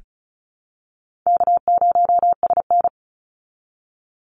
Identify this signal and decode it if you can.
Morse code, heard as a keyed tone, about 34 wpm, 700 Hz. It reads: K0SN
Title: Easy Call Signs Repeated 3x - 35wpm